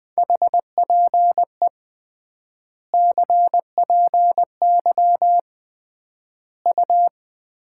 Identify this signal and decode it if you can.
Morse code, heard as a keyed tone, about 20 wpm, 700 Hz. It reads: HPE CPY U
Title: Common QSO Phrases - 3x Word-Spacing - 20wpm